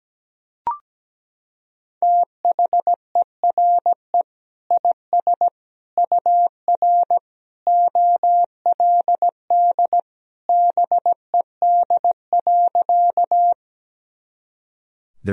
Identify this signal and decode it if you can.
Morse code, heard as a keyed tone, about 17 wpm, 700 Hz. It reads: THERE IS UR OLD BED.
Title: Sentences from Top 500 Words and Abbreviations 17wpm